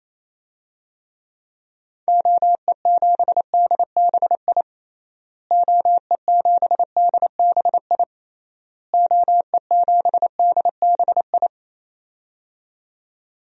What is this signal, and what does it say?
Morse code, heard as a keyed tone, about 28 wpm, 700 Hz. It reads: OE7DBI OE7DBI OE7DBI
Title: DXCC-100 Call Signs - 6 Characters - Repeated 3x - 28wpm